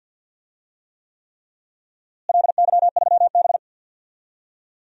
Morse code, heard as a keyed tone, about 52 wpm, 700 Hz. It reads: LY3B
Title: DXCC-100 Call Signs - 4 Characters - Repeated 3x - 50wpm